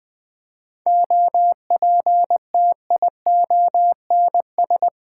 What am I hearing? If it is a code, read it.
Morse code, heard as a keyed tone, about 20 wpm, 700 Hz. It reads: OPTIONS
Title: Sentences with 6-Word Phrases - Spoken Beginning - 20wpm